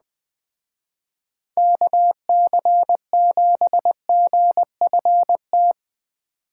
Morse code, heard as a keyed tone, about 20 wpm, 700 Hz. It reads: KC7GFT
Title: Speed Racing - USA 2x3 Call Signs 17wpm